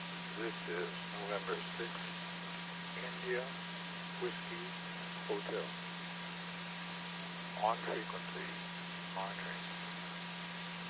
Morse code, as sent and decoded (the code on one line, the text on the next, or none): none